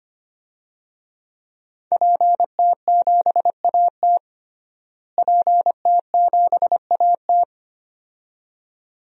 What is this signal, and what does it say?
Morse code, heard as a keyed tone, about 25 wpm, 700 Hz. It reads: PT7AT PT7AT
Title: DXCC-100 Call Signs - 5 Characters - Repeated 2x - 25wpm